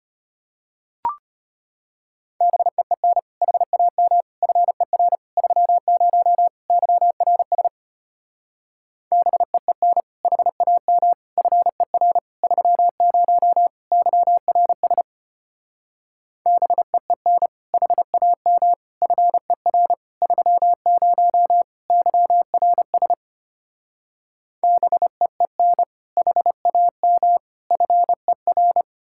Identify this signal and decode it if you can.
Morse code, heard as a keyed tone, about 33 wpm, 700 Hz. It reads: BEEN HAM FER 30 YRS BEEN HAM FER 30 YRS BEEN HAM FER 30 YRS BEEN HAM FER